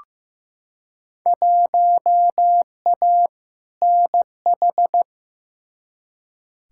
Morse code, heard as a keyed tone, about 15 wpm, 700 Hz. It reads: 1A NH
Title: Field Day Exchanges Encore - 15wpm